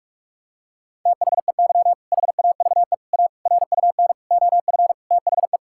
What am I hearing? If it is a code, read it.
Morse code, heard as a keyed tone, about 46 wpm, 690 Hz. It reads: THEY HAVE A RUN OF THE
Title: Sentences from Top 1000 Words - Review Part 5 - 45wpm